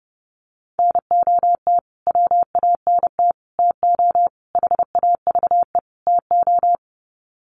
Morse code, heard as a keyed tone, about 30 wpm, 700 Hz. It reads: NOT WANT TO HAVE TO
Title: Sets of 5 Words Encore2 30wpm